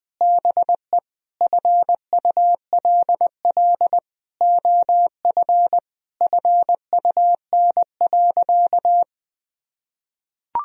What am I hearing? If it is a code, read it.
Morse code, heard as a keyed tone, about 21 wpm, 700 Hz. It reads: BE FULL OF FUN.
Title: Sentences from Top 1000 Words - Review Part 4 - 20wpm